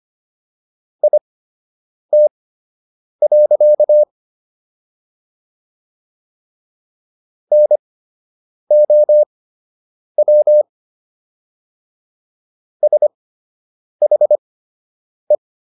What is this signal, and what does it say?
Morse code, heard as a keyed tone, about 25 wpm, spaced as for 8 wpm, 600 Hz. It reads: IT. NOW SHE